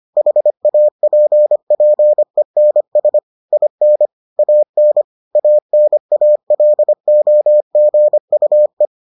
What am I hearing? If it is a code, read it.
Morse code, heard as a keyed tone, about 25 wpm, 600 Hz. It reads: HAPPENS IN AN ANALOGUE